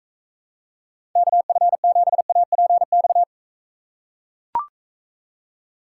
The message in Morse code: KF7APX

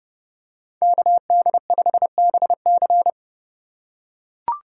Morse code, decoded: KD5BC

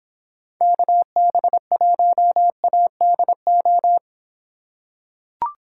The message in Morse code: KB1ADO

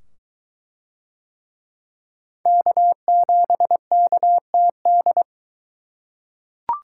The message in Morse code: K7KTD

